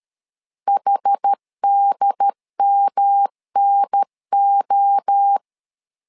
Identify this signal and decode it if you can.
Morse code, heard as a keyed tone, about 13 wpm, 800 Hz. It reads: HDMNO
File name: LEZIONI / 4 Q 12wpm 1833